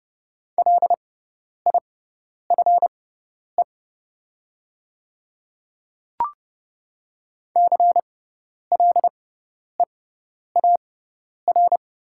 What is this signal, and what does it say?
Morse code, heard as a keyed tone, about 30 wpm, spaced as for 10 wpm, 700 Hz. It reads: LIFE CLEAR
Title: 6x Character Spacing - Top 500 Words - 30wpm